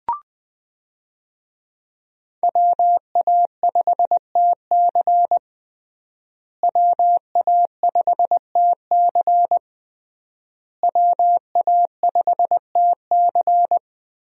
Morse code, heard as a keyed tone, about 20 wpm, 700 Hz. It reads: WA5TC WA5TC WA5TC
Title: USA Call Signs - 5 Characters - Repeated 3x - 20wpm